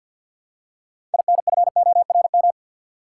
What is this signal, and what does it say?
Morse code, heard as a keyed tone, about 47 wpm, 700 Hz. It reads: INFORM
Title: Six Letter Words - 50wpm